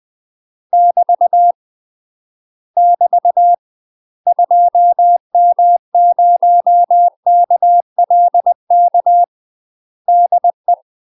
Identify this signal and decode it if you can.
Morse code, heard as a keyed tone, about 20 wpm, 700 Hz. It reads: <BT> <BT> 2M0KLK DE
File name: UK MASTER QSOs TRAINING / 20WPM / UK MASTER QSOs 0015